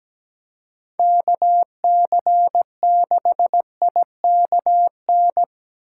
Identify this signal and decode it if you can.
Morse code, heard as a keyed tone, about 20 wpm, 700 Hz. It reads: KC6IKN
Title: Speed Racing - USA Call Signs 17wpm